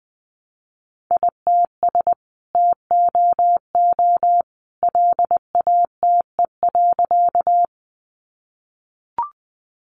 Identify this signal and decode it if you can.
Morse code, heard as a keyed tone, about 20 wpm, 700 Hz. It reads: ITS TOO LATE.
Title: Sentences from Top 600 Words 20wpm